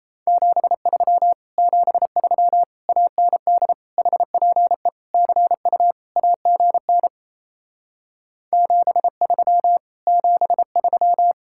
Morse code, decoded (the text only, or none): 73 73 AND HPE CU AGN 73 73